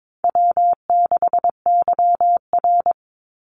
Morse code, heard as a keyed tone, about 22 wpm, 700 Hz. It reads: W6YR
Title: Easy Call Signs Repeated 3x - 22wpm